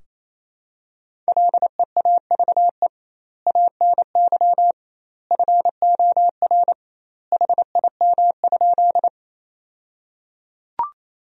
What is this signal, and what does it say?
Morse code, heard as a keyed tone, about 27 wpm, 700 Hz. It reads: LEAVE ANY FOR HIM?
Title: 2x Word-Spacing - Sentences from Top 400 Words - 28wpm